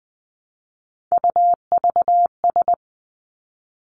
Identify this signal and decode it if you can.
Morse code, heard as a keyed tone, about 19 wpm, 700 Hz. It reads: UVS
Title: Three Challenging Letter-Number Combinations 20wpm